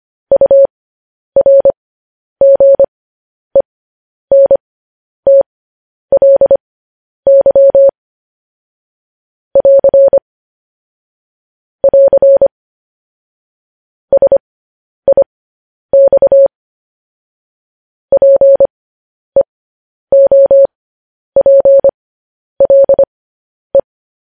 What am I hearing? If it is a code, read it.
Morse code, heard as a keyed tone, about 25 wpm, spaced as for 10 wpm, 550 Hz. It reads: URGENTLY <AR> <AR> SIX PEOPLE
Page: news-headlines-in-morse-code-at-10-wpm